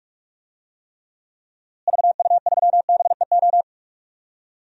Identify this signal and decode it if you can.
Morse code, heard as a keyed tone, about 46 wpm, 700 Hz. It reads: VU3BEO
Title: DXCC-100 Call Signs - 6 Characters - Repeated 2x - 45wpm